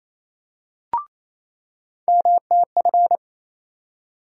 Morse code, heard as a keyed tone, about 27 wpm, 700 Hz. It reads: MTF
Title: USA Call Sign Postfixes - 3 Letters 28wpm